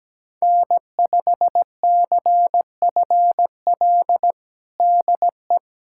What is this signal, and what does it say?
Morse code, heard as a keyed tone, about 17 wpm, 700 Hz. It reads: N5CFL DE
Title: USA Callsign DE Callsign - 5 Characters - 17wpm